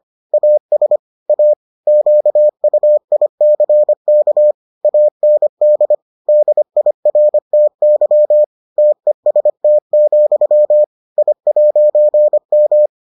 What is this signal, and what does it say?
Morse code, heard as a keyed tone, about 25 wpm, 600 Hz. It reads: AS A QUICK AND DIRTY TEST, I'M